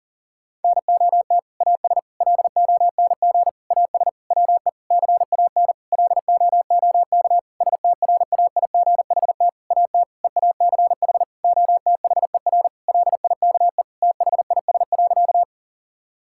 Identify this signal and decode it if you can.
Morse code, heard as a keyed tone, about 40 wpm, 690 Hz. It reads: NOT AS LONG AS WE CAN LOOK STRAIGHT AT EACH OTHER LIKE THIS.
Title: Sentences from Top 1000 Words - Review Part 6 - 40wpm